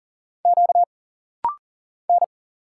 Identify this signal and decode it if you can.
Morse code, heard as a keyed tone, about 40 wpm, 700 Hz. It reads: Q N
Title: Single Letters - Rapid-Fire 40wpm